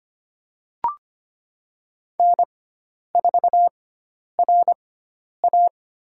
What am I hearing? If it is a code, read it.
Morse code, heard as a keyed tone, about 25 wpm, 700 Hz. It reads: N 4 R A
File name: Lesson 16g - Letter c - Call Signs 25wpm